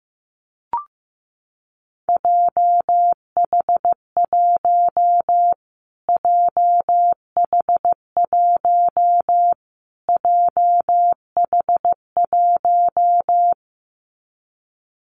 Morse code, heard as a keyed tone, about 15 wpm, 700 Hz. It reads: JH1 JH1 JH1
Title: Callsign Prefixes for Top 25 DXCC - Repeated 3x - 15wpm